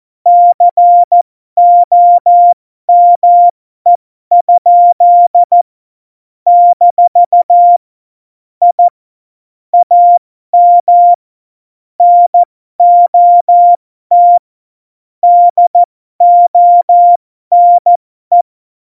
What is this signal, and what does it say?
Morse code, heard as a keyed tone, about 14 wpm, spaced as for 12 wpm, 700 Hz. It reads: COME? - I AM NOT DONE
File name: Morsekurs / Witze